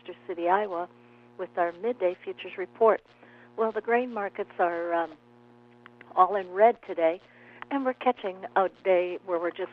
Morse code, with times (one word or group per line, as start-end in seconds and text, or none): none